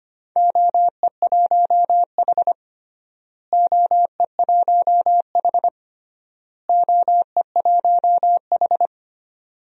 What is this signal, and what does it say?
Morse code, heard as a keyed tone, about 25 wpm, 700 Hz. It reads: OE1H OE1H OE1H
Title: DXCC-100 Call Signs - 4 Characters - Repeated 3x - 25wpm